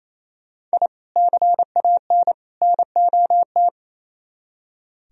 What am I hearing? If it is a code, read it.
Morse code, heard as a keyed tone, about 34 wpm, 700 Hz. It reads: I CAN NOT
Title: Speed-Racing - Sets of 3 Words 28wpm